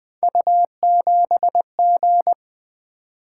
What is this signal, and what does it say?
Morse code, heard as a keyed tone, about 20 wpm, 700 Hz. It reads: U7G
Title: Three Letter-Number Combinations - 20wpm